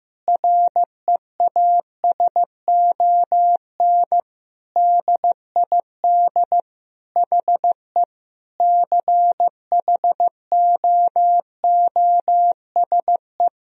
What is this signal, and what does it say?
Morse code, heard as a keyed tone, about 15 wpm, 700 Hz. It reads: REASON DID HE CHOOSE
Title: Sentences from Top 1000 Words - Review Part 4 - 15wpm